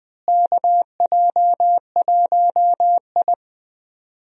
Morse code, heard as a keyed tone, about 21 wpm, 700 Hz. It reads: KJ1I
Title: USA Call Signs - 4 Characters - Repeated 3x - 20wpm